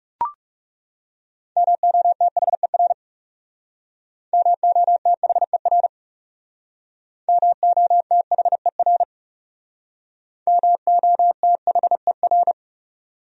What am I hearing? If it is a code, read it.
Morse code, heard as a keyed tone, about 37 wpm, 700 Hz. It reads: MOTHER MOTHER MOTHER MOTHER
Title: Speed-Racing Top 300 Words 30wpm